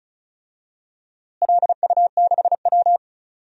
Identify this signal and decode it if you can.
Morse code, heard as a keyed tone, about 34 wpm, 700 Hz. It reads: LU6W
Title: DXCC-100 Call Signs - 4 Characters - Repeated 3x - 35wpm